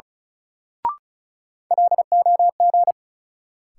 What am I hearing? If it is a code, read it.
Morse code, heard as a keyed tone, about 36 wpm, 700 Hz. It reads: LOG
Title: Sentences from Top 1400 Words - New Word Drill - 35wpm